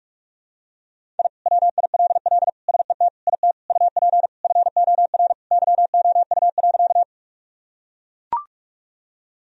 Morse code, heard as a keyed tone, about 43 wpm, 700 Hz. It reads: I WILL SET IT UP FOR YOU.
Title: Sentences from Top 1000 Words - Review Part 1 - 45wpm